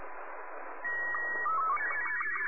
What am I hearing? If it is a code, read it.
SSTV (slow-scan television), Robot 36.
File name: sat / RADIOSCAF-B 24Dec2011